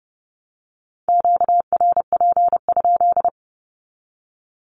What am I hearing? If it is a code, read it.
Morse code, heard as a keyed tone, about 30 wpm, 700 Hz. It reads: QRP?
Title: Q-Codes 30wpm